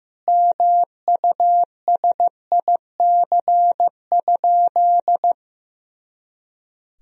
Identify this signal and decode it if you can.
Morse code, heard as a keyed tone, about 15 wpm, 700 Hz. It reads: MUSIC?
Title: Sentences from Top 300 Words and Abbreviations 15wpm